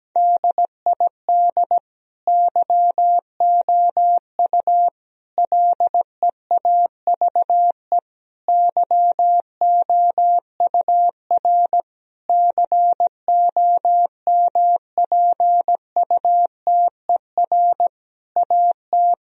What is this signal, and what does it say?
Morse code, heard as a keyed tone, about 17 wpm, 700 Hz. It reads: DID YOU LEAVE YOUR COMPUTER AT
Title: Sentences from Top 1000 Words - Review Part 3 - 17wpm